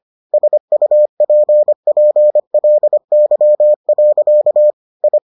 Morse code, heard as a keyed tone, about 25 wpm, 600 Hz. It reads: SUPPLY. I